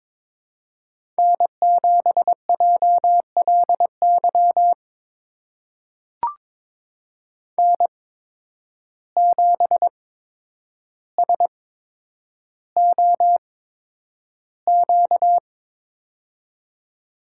Morse code, heard as a keyed tone, about 22 wpm, spaced as for 10 wpm, 700 Hz. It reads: N7JLY N7SOQ
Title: USA Call Signs - 5 Characters - 8x Character Spacing - 22wpm